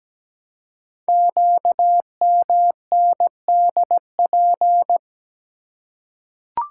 Morse code, heard as a keyed tone, about 17 wpm, 700 Hz. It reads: QMNDP
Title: Code Groups - 5 Letters - 17wpm